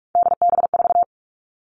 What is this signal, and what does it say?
Morse code, heard as a keyed tone, about 44 wpm, 700 Hz. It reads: DB4
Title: All Cut Numbers - 3-Digit 45wpm